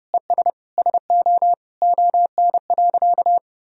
Morse code, heard as a keyed tone, about 30 wpm, 700 Hz. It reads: ES SO ON.
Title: Sentences from Top 100 Words and Abbreviations 30wpm